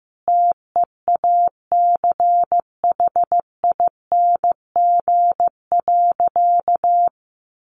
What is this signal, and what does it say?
Morse code, heard as a keyed tone, about 15 wpm, 700 Hz. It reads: TEACHING.
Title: Sentences from Top 1100 Words 15wpm